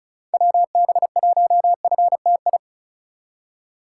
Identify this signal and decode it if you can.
Morse code, heard as a keyed tone, about 36 wpm, 700 Hz. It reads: WB1FTI
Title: USA Call Signs - 6 Characters - 3x Character Spacing - 35wpm